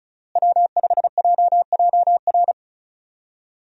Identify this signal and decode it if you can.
Morse code, heard as a keyed tone, about 34 wpm, 700 Hz. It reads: W5JJR
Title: USA Call Signs - 5 Characters - Repeated 3x - 35wpm